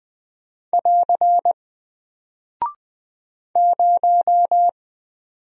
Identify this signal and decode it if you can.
Morse code, heard as a keyed tone, about 20 wpm, 700 Hz. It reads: <AR> 0
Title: Single Character (Letter:Number:Punctuation:Prosign) Alt 20wpm